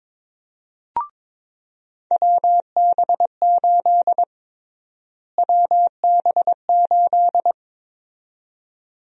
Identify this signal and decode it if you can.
Morse code, heard as a keyed tone, about 22 wpm, 700 Hz. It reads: WB8 WB8
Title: USA Call Sign Prefixes - Repeated 2x - 22wpm